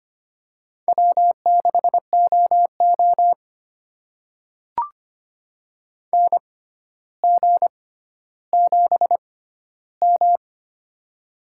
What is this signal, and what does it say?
Morse code, heard as a keyed tone, about 25 wpm, spaced as for 13 wpm, 700 Hz. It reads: W6OO NG7M
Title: USA Call Signs - 4 Characters - 6x Character Spacing - 25wpm